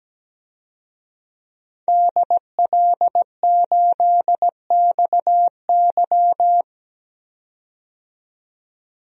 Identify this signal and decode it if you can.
Morse code, heard as a keyed tone, about 17 wpm, 700 Hz. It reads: DL8XY
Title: DXCC-100 Call Signs - 5 Characters - 17wpm